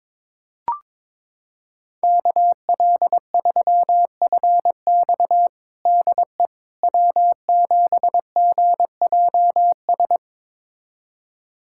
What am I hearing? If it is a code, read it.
Morse code, heard as a keyed tone, about 22 wpm, 700 Hz. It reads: KL3FX DE W7GJS
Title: USA Callsign DE Callsign - 5 Characters - 22wpm